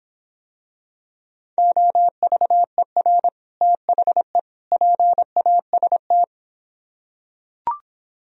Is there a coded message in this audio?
Yes: Morse code from 1.6 to 6.2 s, heard as a keyed tone, about 26 wpm, 700 Hz. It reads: OVER THE PAST